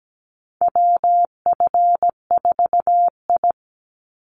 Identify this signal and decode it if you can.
Morse code, heard as a keyed tone, about 17 wpm, 700 Hz. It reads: WF4I
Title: NA SOTA Chaser Callsigns 17wpm